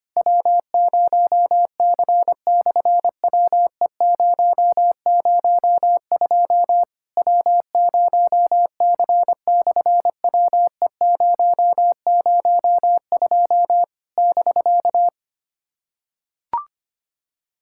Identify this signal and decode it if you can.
Morse code, heard as a keyed tone, about 25 wpm, 700 Hz. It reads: W0C/WE002 W0C/WE002 <BK>
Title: SOTA QSO Practice 25wpm